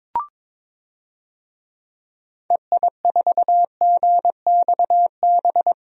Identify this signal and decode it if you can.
Morse code, heard as a keyed tone, about 22 wpm, 700 Hz. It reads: EI4GXB
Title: DXCC-100 Call Signs - 6 Characters - 22wpm